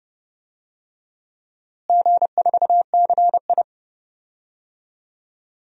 Morse code, heard as a keyed tone, about 30 wpm, 700 Hz. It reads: G4CI
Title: DXCC-100 Call Signs - 4 Characters - Repeated 2x - 30wpm